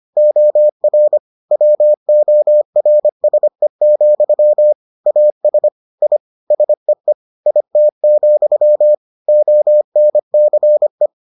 Morse code, heard as a keyed tone, about 25 wpm, 600 Hz. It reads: OR WORSE, AS I SEE IT, ONCE